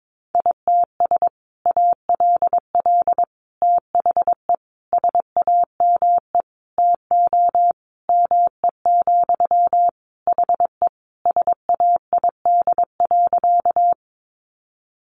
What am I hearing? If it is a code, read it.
Morse code, heard as a keyed tone, about 22 wpm, 700 Hz. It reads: ITS ALL THE SAME TO ME, HE SAID.
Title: Sentences from Top 600 Words 22wpm